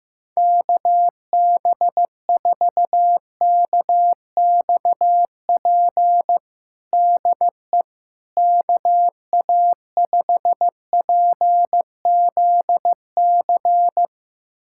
Morse code, heard as a keyed tone, about 15 wpm, 700 Hz. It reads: KB4KXP DE KA5PZC
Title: USA Callsign DE Callsign - 6 Characters - 15wpm